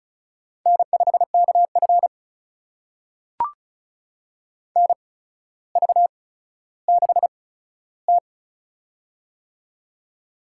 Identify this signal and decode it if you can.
Morse code, heard as a keyed tone, about 35 wpm, spaced as for 15 wpm, 700 Hz. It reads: N5KF NV6T
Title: USA Call Signs - 4 Characters - 8x Character Spacing - 35wpm